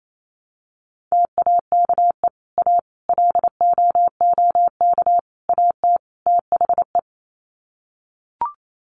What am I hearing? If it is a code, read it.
Morse code, heard as a keyed tone, about 28 wpm, 700 Hz. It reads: TAKE A LOOK AT THE
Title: Sets of 5 Words 28wpm